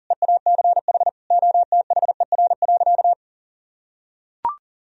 Morse code, heard as a keyed tone, about 40 wpm, 700 Hz. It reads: EACH OTHER.
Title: Sentences from Top 1000 Words - Review Part 1 - 40wpm